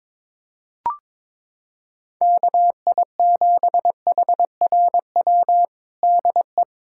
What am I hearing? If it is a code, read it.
Morse code, heard as a keyed tone, about 22 wpm, 700 Hz. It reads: KI7HRW DE
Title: USA Callsign DE Callsign - 6 Characters - 22wpm